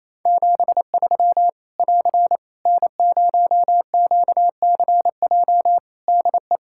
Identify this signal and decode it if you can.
Morse code, heard as a keyed tone, about 28 wpm, 700 Hz. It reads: 73 <AR> N0QCJ DE
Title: Practice QSO - Exchange 4 - Ending - 28wpm